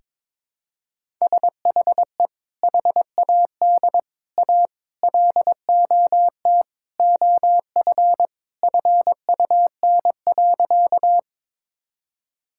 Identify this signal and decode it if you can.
Morse code, heard as a keyed tone, about 22 wpm, 700 Hz. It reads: SHE HAD A LOT OF FUN.
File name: practice / 22wpm / Sentences with 4-Word Phrases - 22wpm